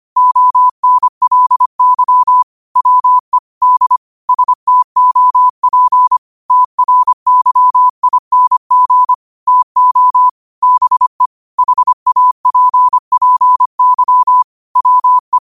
Morse code, heard as a keyed tone, about 25 wpm, 1000 Hz. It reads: ONLY WE D STOP TRYING TO BE HAPPY WE
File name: 25wpm